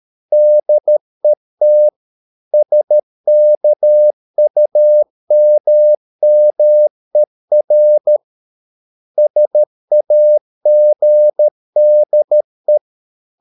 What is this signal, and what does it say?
Morse code, heard as a keyed tone, about 13 wpm, 600 Hz. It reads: DET SKUMMER SAGDE
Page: Ferie 13 wpm